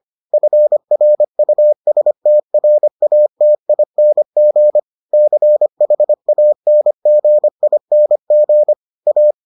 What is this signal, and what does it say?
Morse code, heard as a keyed tone, about 25 wpm, 600 Hz. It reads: FRUSTRATING CHANGING A